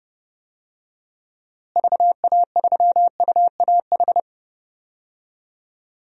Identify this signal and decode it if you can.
Morse code, heard as a keyed tone, about 32 wpm, 700 Hz. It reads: VA3UAH
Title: DXCC-100 Call Signs - 6 Characters - Repeated 3x - 30wpm